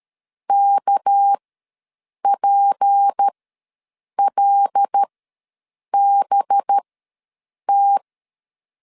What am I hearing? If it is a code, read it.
Morse code, heard as a keyed tone, about 13 wpm, spaced as for 7 wpm, 800 Hz. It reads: KPLBT
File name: LEZIONI / 6 PKX0 / 6 PKX0 ws 12wpm 3448